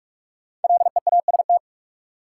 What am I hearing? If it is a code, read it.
Morse code, heard as a keyed tone, about 45 wpm, 700 Hz. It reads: LEAST
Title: Sentences from Top 700 Words - New Word Drill - 45wpm